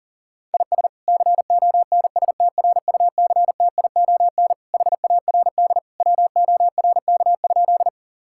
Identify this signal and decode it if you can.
Morse code, heard as a keyed tone, about 40 wpm, 700 Hz. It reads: IS CONSTRUCTION HARD WORK?
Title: Sentences from Top 1600 Words 40wpm